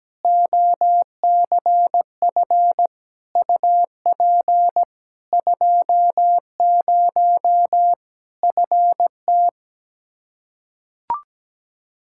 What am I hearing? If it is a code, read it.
Morse code, heard as a keyed tone, about 17 wpm, 700 Hz. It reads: OCF UP 20 FT
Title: K1USN SST QSO Practice 17wpm